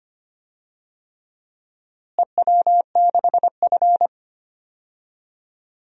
Morse code, heard as a keyed tone, about 25 wpm, 700 Hz. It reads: EW6F
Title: DXCC-100 Call Signs - 4 Characters - Repeated 3x - 25wpm